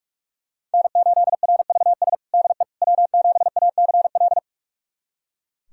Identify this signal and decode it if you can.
Morse code, heard as a keyed tone, about 44 wpm, 700 Hz. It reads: N8RVS DE W7ACL
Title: USA Callsign DE Callsign - 5 Characters - 45wpm